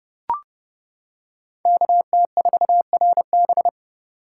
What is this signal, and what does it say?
Morse code, heard as a keyed tone, about 30 wpm, 700 Hz. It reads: KT4RB